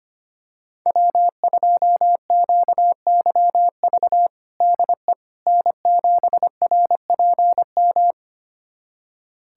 Morse code, heard as a keyed tone, about 25 wpm, 700 Hz. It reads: W2QYV DE N7RPM
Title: USA Callsign DE Callsign - 5 Characters - 25wpm